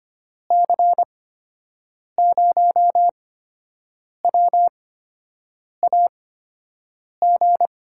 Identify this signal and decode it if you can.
Morse code, heard as a keyed tone, about 25 wpm, spaced as for 7 wpm, 700 Hz. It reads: C0WAG
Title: USA Call Signs - 6 Characters - 8x Character Spacing - 25wpm